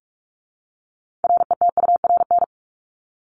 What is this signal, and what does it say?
Morse code, heard as a keyed tone, about 44 wpm, 700 Hz. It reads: RETURN
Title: ED Suffix - Top 100 Words with Base Words 45wpm